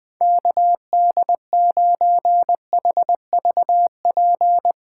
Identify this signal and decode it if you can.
Morse code, heard as a keyed tone, about 20 wpm, 700 Hz. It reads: KD9HVP